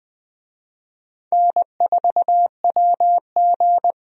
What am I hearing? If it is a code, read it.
Morse code, heard as a keyed tone, about 20 wpm, 700 Hz. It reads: N4WG
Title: USA Call Signs - 4 Characters - 20wpm